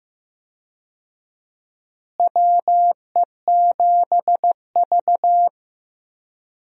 Morse code, heard as a keyed tone, about 15 wpm, 700 Hz. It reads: WE7V
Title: USA Call Signs - 4 Characters - Repeated 2x - 15wpm